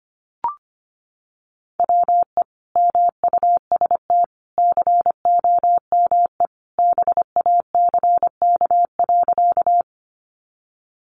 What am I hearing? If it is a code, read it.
Morse code, heard as a keyed tone, about 25 wpm, 700 Hz. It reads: WE MUST COME BACK.
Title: Sentences from Top 200 Words 25wpm